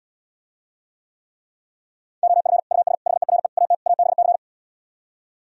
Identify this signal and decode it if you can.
Morse code, heard as a keyed tone, about 77 wpm, 700 Hz. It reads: OF US HERE AT ABC